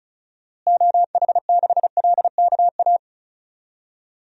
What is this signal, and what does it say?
Morse code, heard as a keyed tone, about 36 wpm, 700 Hz. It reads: OH6LKA
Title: DXCC-100 Call Signs - 6 Characters - Repeated 3x - 35wpm